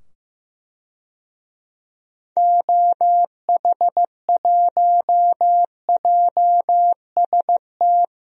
Morse code, heard as a keyed tone, about 15 wpm, 700 Hz. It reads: OH1JST
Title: DXCC-100 Call Signs - 6 Characters - 15wpm